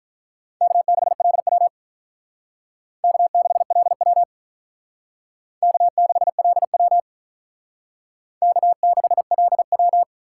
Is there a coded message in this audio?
Yes: Morse code from 0.6 to 10.0 s, heard as a keyed tone, about 44 wpm, 700 Hz. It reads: K6LW K6LW K6LW K6LW